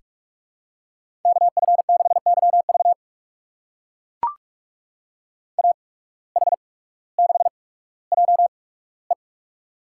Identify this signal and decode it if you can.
Morse code, heard as a keyed tone, about 44 wpm, spaced as for 19 wpm, 700 Hz. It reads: KF6YV AH6JE
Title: USA Call Signs - 5 Characters - 8x Character Spacing - 45wpm